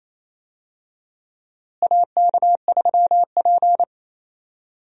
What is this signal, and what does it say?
Morse code, heard as a keyed tone, about 28 wpm, 700 Hz. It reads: AK3P
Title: USA Call Signs - 4 Characters - Repeated 2x - 28wpm